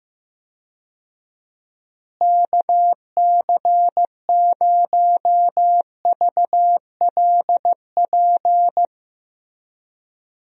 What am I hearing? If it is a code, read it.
Morse code, heard as a keyed tone, about 15 wpm, 700 Hz. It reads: KC0VLP
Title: USA Call Signs - 6 Characters - Repeated 2x - 15wpm